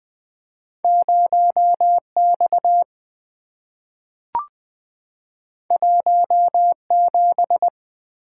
Morse code, heard as a keyed tone, about 20 wpm, 700 Hz. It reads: 0 X 1 7
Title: Two Letter-Number Combinations - 20wpm